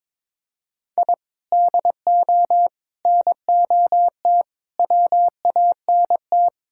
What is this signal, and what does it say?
Morse code, heard as a keyed tone, about 22 wpm, 700 Hz. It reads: I DO NOT WANT
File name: practice / 22wpm / Build A Phrase - 5-Word Phrases - 22wpm